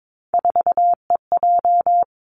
Morse code, heard as a keyed tone, about 22 wpm, 700 Hz. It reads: 4EJ
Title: Three Challenging Letter-Number Combinations 22wpm